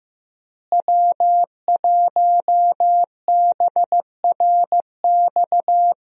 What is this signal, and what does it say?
Morse code, heard as a keyed tone, about 15 wpm, 700 Hz. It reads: W1BRX
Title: USA Call Signs - 5 Characters - Repeated 3x - 15wpm